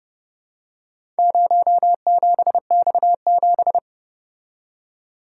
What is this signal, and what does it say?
Morse code, heard as a keyed tone, about 30 wpm, 700 Hz. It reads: 07X7